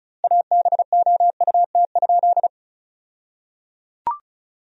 Morse code, heard as a keyed tone, about 34 wpm, 700 Hz. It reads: ABOUT?
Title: 2x Word-Spacing - Sentences from Top 400 Words - 35wpm